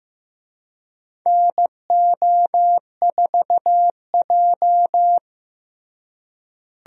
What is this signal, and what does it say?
Morse code, heard as a keyed tone, about 15 wpm, 700 Hz. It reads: NO4J
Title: USA Call Signs - 4 Characters - Repeated 3x - 15wpm